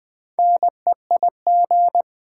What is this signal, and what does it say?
Morse code, heard as a keyed tone, about 20 wpm, 700 Hz. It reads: NEIG
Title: Build A Word - 4 Letters At A Time - 11-Letter Words - 20wpm